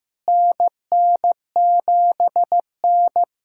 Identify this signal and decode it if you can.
Morse code, heard as a keyed tone, about 15 wpm, 700 Hz. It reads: NN7N
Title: USA Callsign DE Callsign - 4 Characters - 15wpm